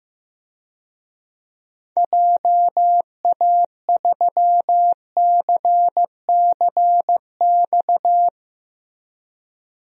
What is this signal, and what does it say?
Morse code, heard as a keyed tone, about 15 wpm, 700 Hz. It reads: JA3CCX
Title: DXCC-100 Call Signs - 6 Characters - Repeated 3x - 15wpm